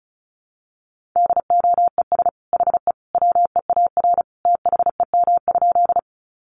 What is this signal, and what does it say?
Morse code, heard as a keyed tone, about 34 wpm, 700 Hz. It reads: DOES HE WEAR THEM?
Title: Sentences from Top 1500 Words 35wpm